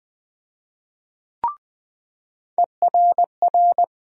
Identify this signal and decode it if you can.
Morse code, heard as a keyed tone, about 20 wpm, 700 Hz. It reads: ERR
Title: USA Call Sign Postfixes - 3 Letters 20wpm